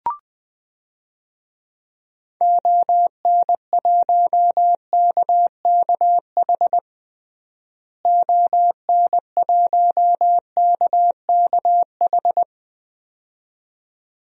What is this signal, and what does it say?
Morse code, heard as a keyed tone, about 20 wpm, 700 Hz. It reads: ON1KKH ON1KKH
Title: DXCC-100 Call Signs - 6 Characters - Repeated 2x - 20wpm